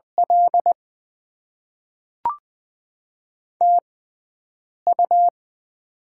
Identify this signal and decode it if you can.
Morse code, heard as a keyed tone, about 20 wpm, spaced as for 7 wpm, 700 Hz. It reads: L TU
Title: Top 20 QSO Elements - 6x Character Spacing - 20wpm